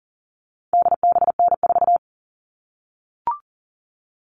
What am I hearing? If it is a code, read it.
Morse code, heard as a keyed tone, about 40 wpm, 700 Hz. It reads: DBN4
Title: All Cut Numbers - 4-Digit 40wpm